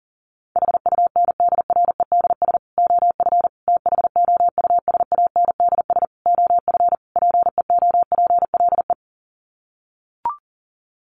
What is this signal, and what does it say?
Morse code, heard as a keyed tone, about 37 wpm, 700 Hz. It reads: HUNDREDS OF THOUSANDS OF PEOPLE
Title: Sets of 5 Words Encore2 40wpm